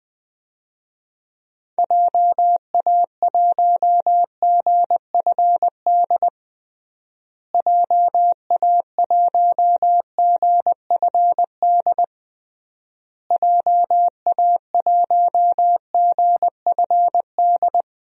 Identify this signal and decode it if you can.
Morse code, heard as a keyed tone, about 21 wpm, 700 Hz. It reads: JA1GFD JA1GFD JA1GFD
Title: DXCC-100 Call Signs - 6 Characters - Repeated 3x - 20wpm